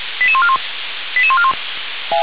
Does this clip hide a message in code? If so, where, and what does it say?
Yes: five-tone selective call (ZVEI) from 0.2 to 1.5 s, heard as quick runs of five tones, tones of about 70 ms. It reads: 88141 88141